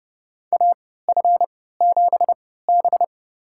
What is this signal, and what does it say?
Morse code, heard as a keyed tone, about 30 wpm, spaced as for 17 wpm, 700 Hz. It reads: AF7B
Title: USA Call Signs - 4 Characters - 3x Character Spacing - 30wpm